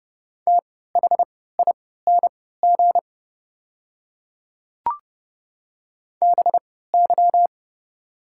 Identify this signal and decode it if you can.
Morse code, heard as a keyed tone, about 30 wpm, spaced as for 17 wpm, 700 Hz. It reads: THING BY
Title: 3x Character Spacing - Top 100 Words - 30wpm